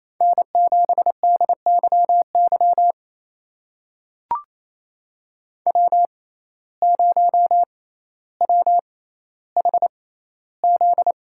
Morse code, heard as a keyed tone, about 28 wpm, spaced as for 14 wpm, 700 Hz. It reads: N7DYY W0WHZ
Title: USA Call Signs - 5 Characters - 6x Character Spacing - 28wpm